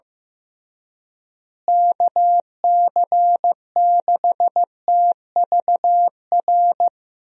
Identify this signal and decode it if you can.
Morse code, heard as a keyed tone, about 15 wpm, 700 Hz. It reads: KC6TVR
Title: USA Call Signs - 6 Characters - Repeated 3x - 15wpm